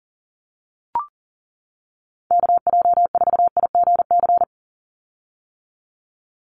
Morse code, heard as a keyed tone, about 41 wpm, 700 Hz. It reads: KJ4IGC
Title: USA Call Signs 40wpm